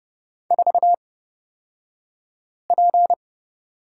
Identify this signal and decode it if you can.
Morse code, heard as a keyed tone, about 30 wpm, spaced as for 5 wpm, 700 Hz. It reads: 4P
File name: Lesson 37d - Letter x - Two Characters 30wpm